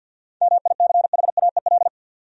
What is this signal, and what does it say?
Morse code, heard as a keyed tone, about 47 wpm, 700 Hz. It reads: MICHAEL
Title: USA Common Names - 50wpm